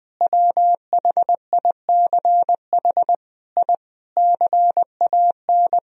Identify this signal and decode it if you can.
Morse code, heard as a keyed tone, about 20 wpm, 700 Hz. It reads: WHICH I CAN
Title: Sentences from Top 1000 Words - Review Part 2 - 20wpm